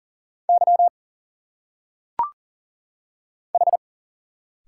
Morse code, heard as a keyed tone, about 40 wpm, 700 Hz. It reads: Y H
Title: Single Letters - Phonetic Pronunciation 40wpm